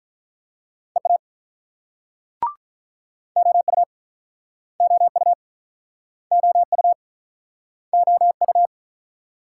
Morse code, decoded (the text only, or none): EA OU OU OU OU